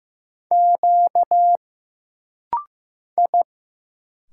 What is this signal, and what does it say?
Morse code, heard as a keyed tone, about 15 wpm, 700 Hz. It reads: Q I
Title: Single Letters - Rapid-Fire 15wpm